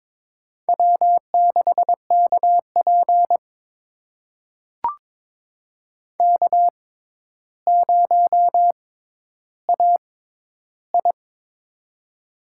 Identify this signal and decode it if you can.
Morse code, heard as a keyed tone, about 22 wpm, spaced as for 11 wpm, 700 Hz. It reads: W6KP K0AI
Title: USA Call Signs - 4 Characters - 6x Character Spacing - 22wpm